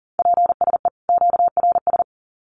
Morse code, heard as a keyed tone, about 40 wpm, 700 Hz. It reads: PSE QRS
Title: Common QSO Phrases 40wpm